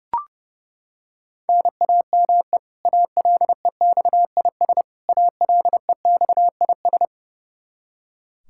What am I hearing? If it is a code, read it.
Morse code, heard as a keyed tone, about 32 wpm, 700 Hz. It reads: NAME ALEXIS ALEXIS
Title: K1USN SST QSO Practice 30wpm